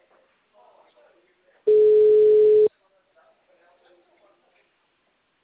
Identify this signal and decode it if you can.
Ringback tone (Europe, 425 Hz), with speech.
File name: out- -2012- - - .41344